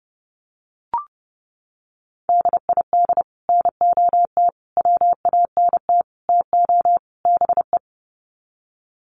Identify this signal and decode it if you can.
Morse code, heard as a keyed tone, about 30 wpm, 700 Hz. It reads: DID NOT WANT TO BE
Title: Sets of 5 Words Encore 30wpm